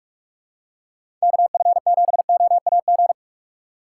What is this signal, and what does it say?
Morse code, heard as a keyed tone, about 45 wpm, 700 Hz. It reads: KF7OAG
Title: USA Call Signs - 6 Characters - 3x Character Spacing - 45wpm